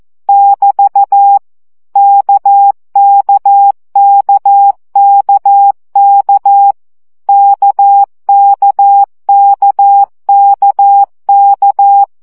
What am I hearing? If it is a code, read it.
Morse code, heard as a keyed tone, about 14 wpm, 800 Hz. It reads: <BT> KKKKK KKKKK
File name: CW gyakorlas k 12 baud